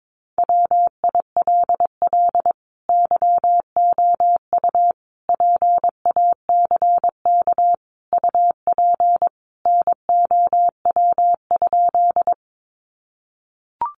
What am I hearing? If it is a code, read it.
Morse code, heard as a keyed tone, about 22 wpm, 700 Hz. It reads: WILL YOU PACK UP NOW?
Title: Sentences from Top 1300 Words 22wpm